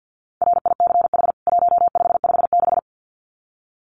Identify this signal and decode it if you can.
Morse code, heard as a keyed tone, about 50 wpm, 700 Hz. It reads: RICH 1556
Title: CWT Exchange 50wpm